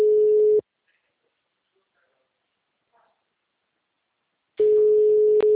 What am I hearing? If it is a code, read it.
Ringback tone (Europe, 425 Hz).
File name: out- -2026- - - .10